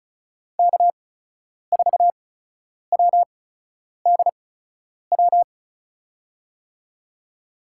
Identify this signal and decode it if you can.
Morse code, heard as a keyed tone, about 36 wpm, spaced as for 10 wpm, 700 Hz. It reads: K4WDW
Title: USA Call Signs - 8x Character Spacing - 35wpm